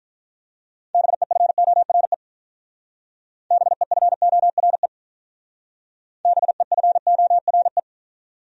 Morse code, heard as a keyed tone, about 42 wpm, 700 Hz. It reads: BEFORE BEFORE BEFORE
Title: Speed-Racing Top 300 Words 35wpm